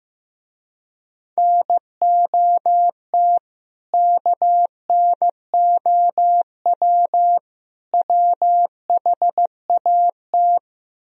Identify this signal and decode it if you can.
Morse code, heard as a keyed tone, about 15 wpm, 700 Hz. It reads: NOT KNOW WHAT
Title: Speed-Racing - Sets of 3 Words 15wpm